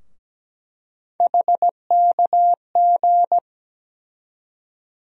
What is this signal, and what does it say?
Morse code, heard as a keyed tone, about 17 wpm, 700 Hz. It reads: HKG
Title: USA Call Sign Postfixes - 3 Letters 17wpm